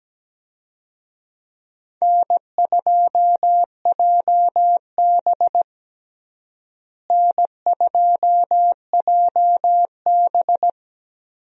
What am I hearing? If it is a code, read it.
Morse code, heard as a keyed tone, about 17 wpm, 700 Hz. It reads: N2JB N2JB
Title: USA Call Signs - 4 Characters - Repeated 2x - 17wpm